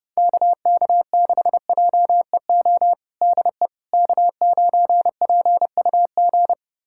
Morse code, heard as a keyed tone, about 30 wpm, 700 Hz. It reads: KK6JEO DE K9PUG
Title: USA Callsign DE Callsign 30wpm